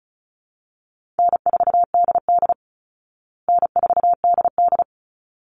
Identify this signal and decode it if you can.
Morse code, heard as a keyed tone, about 34 wpm, 700 Hz. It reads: N4DD N4DD
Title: Easy Call Signs Repeated 2x - 35wpm